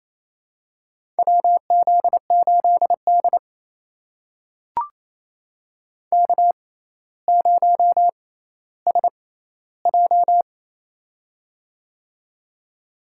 Morse code, heard as a keyed tone, about 28 wpm, spaced as for 14 wpm, 700 Hz. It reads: WZ8D K0SJ
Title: USA Call Signs - 4 Characters - 6x Character Spacing - 28wpm